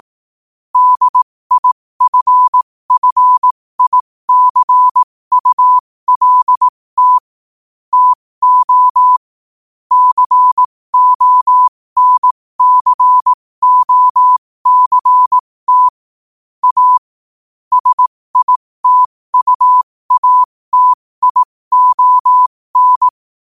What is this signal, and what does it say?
Morse code, heard as a keyed tone, about 18 wpm, spaced as for 15 wpm, 1000 Hz. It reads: DIFFICULT TO CONCOCT A SITUATION
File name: qotdfiles / 15wpm